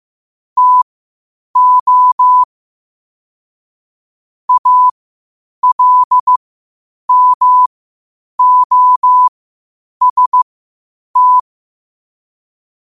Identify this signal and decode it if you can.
Morse code, heard as a keyed tone, about 15 wpm, spaced as for 8 wpm, 1000 Hz. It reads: TO ALMOST
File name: qotdfiles / 8wpm